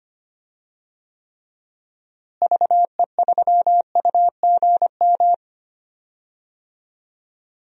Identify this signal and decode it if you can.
Morse code, heard as a keyed tone, about 25 wpm, 700 Hz. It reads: VE3UGM
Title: DXCC-100 Call Signs - 6 Characters - Repeated 2x - 25wpm